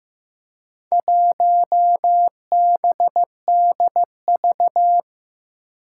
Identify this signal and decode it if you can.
Morse code, heard as a keyed tone, about 15 wpm, 700 Hz. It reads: 1BDV
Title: Four Challenging Letter-Number Combinations 15wpm